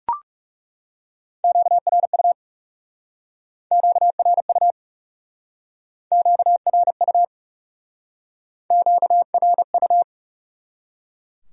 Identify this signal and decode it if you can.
Morse code, heard as a keyed tone, about 37 wpm, 700 Hz. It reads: QRU QRU QRU QRU
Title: Speed-Racing Q-Codes - 30wpm